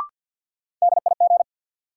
Morse code, heard as a keyed tone, about 47 wpm, 700 Hz. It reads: DIG
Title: Sentences from Top 1600 Words - New Word Drill Alt - 50wpm